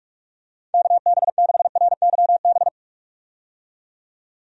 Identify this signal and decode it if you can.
Morse code, heard as a keyed tone, about 46 wpm, 700 Hz. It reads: KB6RYB
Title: USA Call Signs - 6 Characters - 45wpm